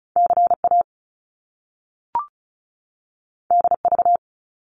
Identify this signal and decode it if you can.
Morse code, heard as a keyed tone, about 36 wpm, 700 Hz. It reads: CA DV
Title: Top Two Letter Combinations in Words 35wpm